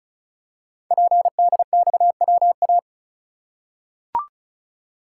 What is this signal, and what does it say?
Morse code, heard as a keyed tone, about 36 wpm, 690 Hz. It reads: PDXWA
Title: Code Groups - 5 Letter:Numbers - 35wpm